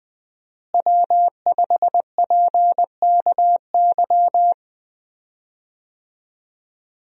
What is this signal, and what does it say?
Morse code, heard as a keyed tone, about 20 wpm, 700 Hz. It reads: W5PKY
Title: USA Call Signs - 5 Characters - Repeated 3x - 20wpm